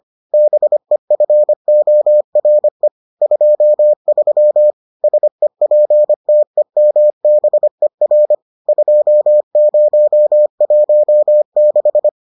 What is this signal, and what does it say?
Morse code, heard as a keyed tone, about 25 wpm, 600 Hz. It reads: BEFORE 23 SEPTEMBER 2016